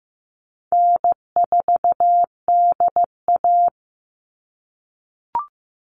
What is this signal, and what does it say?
Morse code, heard as a keyed tone, about 15 wpm, 700 Hz. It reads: N4DA
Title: NA SOTA Chaser Callsigns 15wpm